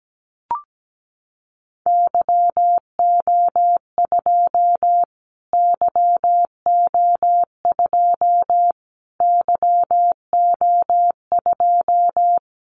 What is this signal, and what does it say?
Morse code, heard as a keyed tone, about 17 wpm, 700 Hz. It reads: YO2 YO2 YO2
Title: Callsign Prefixes for Top 25 DXCC - Repeated 3x - 17wpm